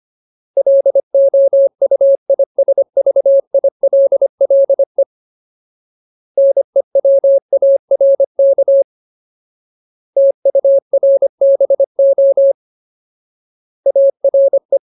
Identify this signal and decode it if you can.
Morse code, heard as a keyed tone, about 25 wpm, 550 Hz. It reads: LOUISVILLE NEWARK TURBO ARE
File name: Ultimate Word List / 25WPM Ultimate